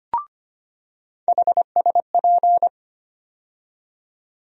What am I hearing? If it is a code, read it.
Morse code, heard as a keyed tone, about 25 wpm, 700 Hz. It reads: HSP
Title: USA Call Sign Postfixes - 3 Letters 25wpm